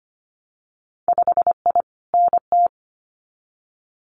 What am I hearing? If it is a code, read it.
Morse code, heard as a keyed tone, about 25 wpm, 700 Hz. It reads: 5I NT
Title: Winter Field Day Exchanges Encore 25wpm